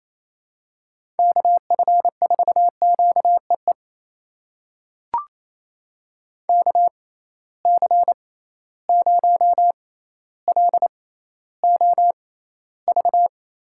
Morse code, heard as a keyed tone, about 28 wpm, spaced as for 14 wpm, 700 Hz. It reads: KF4QEE KC0LOV
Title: USA Call Signs - 6 Characters - 6x Character Spacing - 28wpm